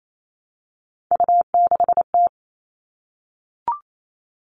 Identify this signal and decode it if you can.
Morse code, heard as a keyed tone, about 28 wpm, 700 Hz. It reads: U6T
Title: All Cut Numbers - 3-Digit 28wpm